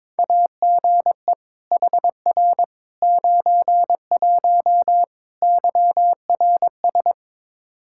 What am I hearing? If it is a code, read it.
Morse code, heard as a keyed tone, about 22 wpm, 700 Hz. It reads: AGE HR 91 YRS
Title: K1USN SST QSO Practice 22wpm